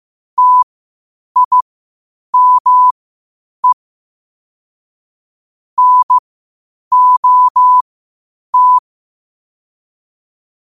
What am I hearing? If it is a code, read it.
Morse code, heard as a keyed tone, about 15 wpm, spaced as for 8 wpm, 1000 Hz. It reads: TIME NOT
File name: qotdfiles / 8wpm